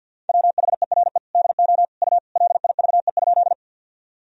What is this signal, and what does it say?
Morse code, heard as a keyed tone, about 53 wpm, 700 Hz. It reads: WHERE DO U LIVE?
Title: Sentences from Top 200 Words and Abbreviations 50wpm